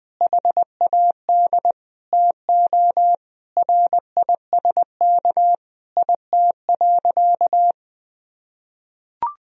Morse code, heard as a keyed tone, about 21 wpm, 700 Hz. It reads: HAD TO RISK IT.
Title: Sentences from Top 1000 Words - Review Part 4 - 20wpm